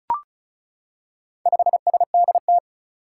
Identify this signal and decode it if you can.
Morse code, heard as a keyed tone, about 34 wpm, 700 Hz. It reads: 5SDT
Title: Four Challenging Letter-Number Combinations 35wpm